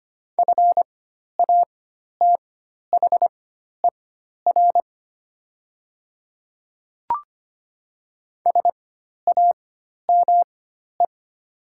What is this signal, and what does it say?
Morse code, heard as a keyed tone, about 25 wpm, spaced as for 12 wpm, 690 Hz. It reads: FATHER SAME
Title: 4x Character Spacing - Top 200 Words - 25wpm